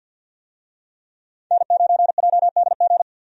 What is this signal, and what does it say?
Morse code, heard as a keyed tone, about 52 wpm, 700 Hz. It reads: N9JDG
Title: USA Call Signs - 5 Characters - 50wpm